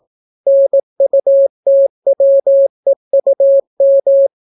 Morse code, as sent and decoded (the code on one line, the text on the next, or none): -. ..- - .-- . ..- --
NUTWEUM